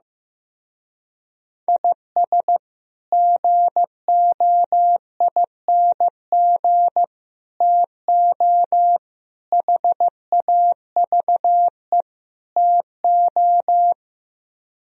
Morse code, heard as a keyed tone, about 15 wpm, 700 Hz. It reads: IS GOING TO HAVE TO
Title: Speed-Racing - Sets of 5 Words 15wpm